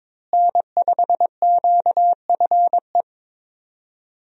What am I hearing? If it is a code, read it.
Morse code, heard as a keyed tone, about 22 wpm, 700 Hz. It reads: N5QFE
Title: USA Call Signs - 5 Characters - Repeated 3x - 22wpm